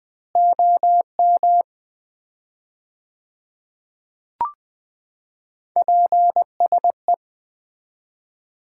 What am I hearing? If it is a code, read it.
Morse code, heard as a keyed tone, about 20 wpm, 700 Hz. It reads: OM PSE
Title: Common QSO Phrases - 4x Word-Spacing - 20wpm